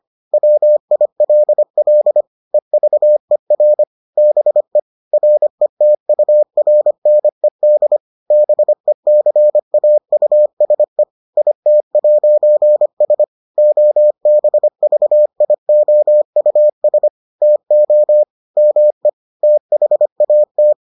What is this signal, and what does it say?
Morse code, heard as a keyed tone, about 25 wpm, 600 Hz. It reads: WILL EVER BE RETURNED BECAUSE IT'S OBVIOUS TO ME THAT